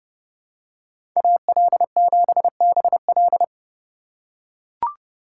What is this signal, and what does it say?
Morse code, heard as a keyed tone, about 31 wpm, 700 Hz. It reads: AL7BL
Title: USA Call Signs - 5 Characters - 3x Character Spacing - 30wpm